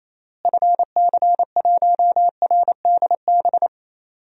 Morse code, heard as a keyed tone, about 28 wpm, 700 Hz. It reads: FC1RDB
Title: DXCC-100 Call Signs - 6 Characters - Repeated 3x - 28wpm